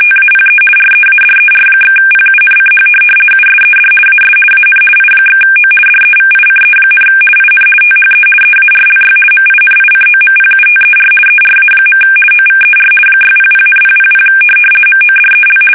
RTTY, about 50 baud. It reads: TAPE READER/PUNCH USED AS AN I/O DEVICE, IN THE EARLY MINICOMPUTER ERA. OVER 600,000 MODEL
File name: fsk850-50B / teletype-corp